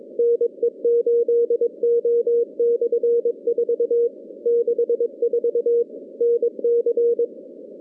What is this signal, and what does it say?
Morse code, heard as a keyed tone, about 22 wpm, 470 Hz. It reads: NE8O/4 64 NC
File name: contests / CW-SS-k5tr- -2100